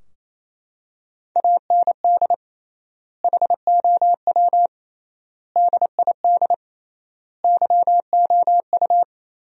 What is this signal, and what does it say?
Morse code, heard as a keyed tone, about 28 wpm, 700 Hz. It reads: AND HOW DID YOU
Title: 3x Word-Spacing - Sentences from Top 100 Words - 28wpm